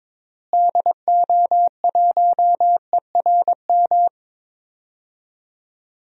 Morse code, heard as a keyed tone, about 22 wpm, 700 Hz. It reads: DO1ERM
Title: DXCC-100 Call Signs - 6 Characters - Repeated 3x - 22wpm